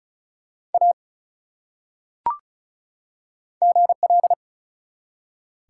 Morse code, heard as a keyed tone, about 34 wpm, 700 Hz. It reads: A GL
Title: Top 180 QSO Elements 35wpm